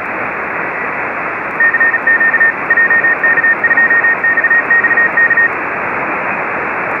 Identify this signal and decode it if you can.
Morse code, heard as a keyed tone, about 36 wpm, 1800 Hz, with radio noise. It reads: CQ JK2XXK